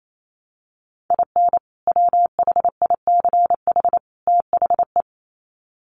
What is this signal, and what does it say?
Morse code, heard as a keyed tone, about 28 wpm, 700 Hz. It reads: IN WHICH THE
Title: Sets of 3 Words 28wpm